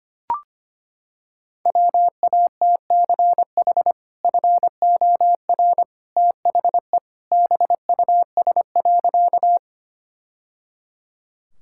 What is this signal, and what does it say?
Morse code, heard as a keyed tone, about 25 wpm, 700 Hz. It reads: WATCH FOR THE BUS.
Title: Sentences from Top 1000 Words - Review Part 6 - 25wpm